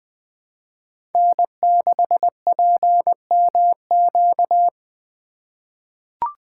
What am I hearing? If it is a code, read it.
Morse code, heard as a keyed tone, about 20 wpm, 700 Hz. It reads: N6PMQ
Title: USA Call Signs - 5 Characters - 3x Character Spacing - 20wpm